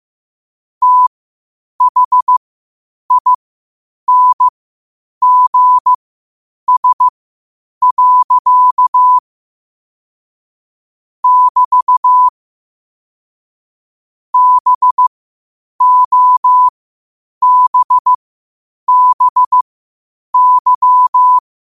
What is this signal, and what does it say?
Morse code, heard as a keyed tone, about 15 wpm, spaced as for 8 wpm, 1000 Hz. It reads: THINGS. <BT> BOBBY
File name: qotdfiles / 8wpm